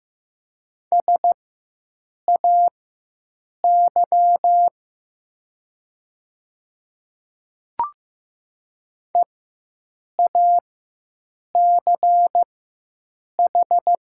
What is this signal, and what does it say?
Morse code, heard as a keyed tone, about 15 wpm, spaced as for 7 wpm, 700 Hz. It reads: SAY EACH
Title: 4x Character Spacing - Top 200 Words - 15wpm